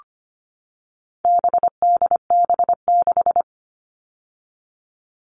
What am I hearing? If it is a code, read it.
Morse code, heard as a keyed tone, about 25 wpm, 700 Hz. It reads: BDB6
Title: All Cut Numbers - 4-Digit 25wpm